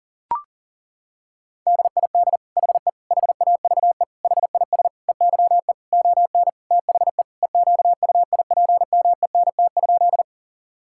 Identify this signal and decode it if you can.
Morse code, heard as a keyed tone, about 40 wpm, 700 Hz. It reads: DID HE HAVE HIS EYE ON THE EQUIPMENT?
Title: Sentences from Top 1000 Words - Review Part 5 - 40wpm